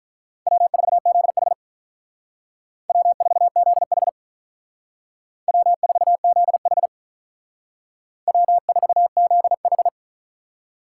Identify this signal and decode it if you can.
Morse code, heard as a keyed tone, about 46 wpm, 700 Hz. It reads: W4ZH W4ZH W4ZH W4ZH